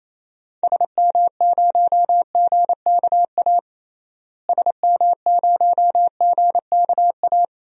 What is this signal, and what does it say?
Morse code, heard as a keyed tone, about 28 wpm, 700 Hz. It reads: SM0GKA SM0GKA
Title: DXCC-100 Call Signs - 6 Characters - Repeated 3x - 28wpm